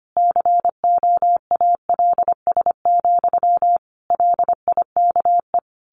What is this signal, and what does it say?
Morse code, heard as a keyed tone, about 25 wpm, 700 Hz. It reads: COALS, LIKE
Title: A Sound Of Thunder 25wpm